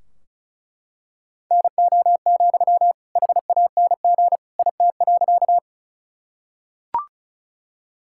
Morse code, heard as a keyed tone, about 34 wpm, 700 Hz. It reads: NO, HANG IT.
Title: Sentences from Top 1600 Words 35wpm